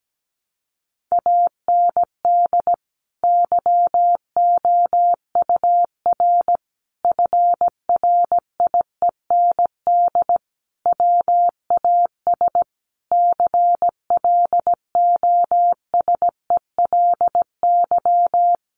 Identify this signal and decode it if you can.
Morse code, heard as a keyed tone, about 17 wpm, 700 Hz. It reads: AND YOUR FRIEND WAS CLOSELY
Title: Sentences from Top 1400 Words 17wpm